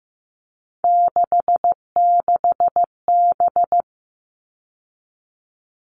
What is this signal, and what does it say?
Morse code, heard as a keyed tone, about 15 wpm, 700 Hz. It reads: 66B
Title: All Cut Numbers - 3-Digit 15wpm